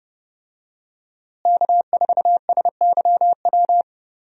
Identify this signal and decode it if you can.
Morse code, heard as a keyed tone, about 30 wpm, 700 Hz. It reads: K4SYW
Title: USA Call Signs - 5 Characters - Repeated 3x - 30wpm